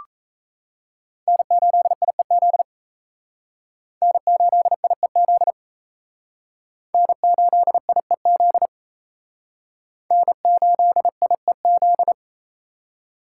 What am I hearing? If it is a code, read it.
Morse code, heard as a keyed tone, about 35 wpm, 700 Hz. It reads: N8IEZ N8IEZ N8IEZ N8IEZ